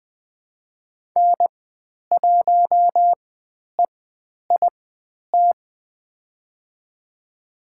Morse code, heard as a keyed tone, about 20 wpm, spaced as for 10 wpm, 700 Hz. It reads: N1EIT
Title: 102 Lesson 17g - Letter u - Call Signs 20wpm